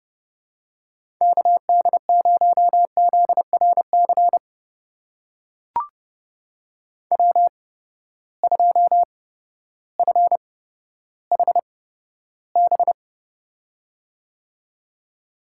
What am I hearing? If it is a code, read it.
Morse code, heard as a keyed tone, about 32 wpm, spaced as for 14 wpm, 700 Hz. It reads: KD0ZRC W2FHB
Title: USA Call Signs - 8x Character Spacing - 30wpm